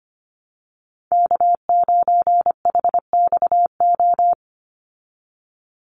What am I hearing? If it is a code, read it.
Morse code, heard as a keyed tone, about 25 wpm, 700 Hz. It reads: K9HXO
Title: Call Signs Repeated 2x - 25wpm